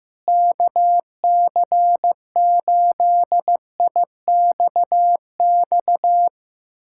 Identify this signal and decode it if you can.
Morse code, heard as a keyed tone, about 15 wpm, 700 Hz. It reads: KC8IXX
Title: USA Callsign DE Callsign 15wpm